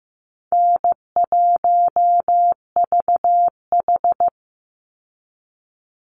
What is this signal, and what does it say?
Morse code, heard as a keyed tone, about 15 wpm, 700 Hz. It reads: N1VH
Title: Easy Call Signs Repeated 2x - 15wpm